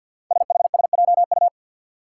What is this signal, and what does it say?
Morse code, heard as a keyed tone, about 51 wpm, 700 Hz. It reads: SHSJU
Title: Five Challenging Letter-Number Combinations 50wpm